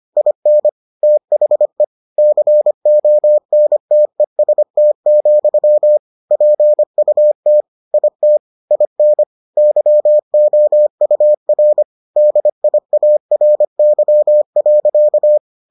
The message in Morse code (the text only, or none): IN THE CONTEST, PUT IT IN YOUR DIARY.